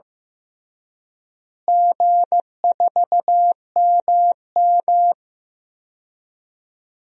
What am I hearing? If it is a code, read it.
Morse code, heard as a keyed tone, about 15 wpm, 700 Hz. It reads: G4MM
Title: DXCC-100 Call Signs - 4 Characters - Repeated 2x - 15wpm